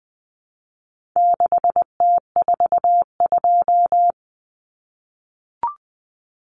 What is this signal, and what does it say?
Morse code, heard as a keyed tone, about 20 wpm, 700 Hz. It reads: 6T42
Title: Common Cut Numbers - 4-Digit 20wpm